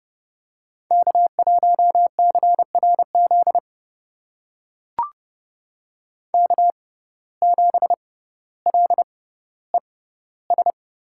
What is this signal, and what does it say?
Morse code, heard as a keyed tone, about 30 wpm, spaced as for 15 wpm, 700 Hz. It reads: K1CRZ K7LES
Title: USA Call Signs - 6x Character Spacing - 30wpm